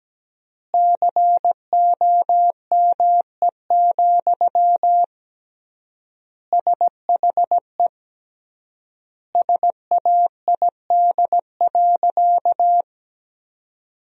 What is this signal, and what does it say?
Morse code, heard as a keyed tone, about 17 wpm, 700 Hz. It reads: COME, SHE SAID.
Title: 3x Word-Spacing - Sentences from Top 100 Words - 17wpm